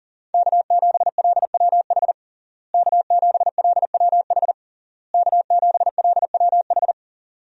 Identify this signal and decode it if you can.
Morse code, heard as a keyed tone, about 40 wpm, 700 Hz. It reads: K7LWH K7LWH K7LWH
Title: USA Call Signs - 5 Characters - Repeated 3x - 40wpm